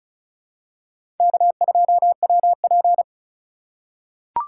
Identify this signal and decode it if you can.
Morse code, heard as a keyed tone, about 36 wpm, 700 Hz. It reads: K2WP
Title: USA Call Signs - 4 Characters - 3x Character Spacing - 35wpm